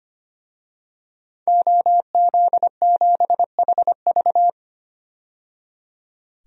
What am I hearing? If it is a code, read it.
Morse code, heard as a keyed tone, about 25 wpm, 700 Hz. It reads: OZ7HV
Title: DXCC-100 Call Signs - 5 Characters - 25wpm